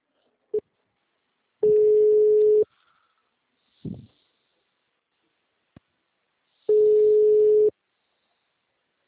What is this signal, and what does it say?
Ringback tone (Europe, 425 Hz).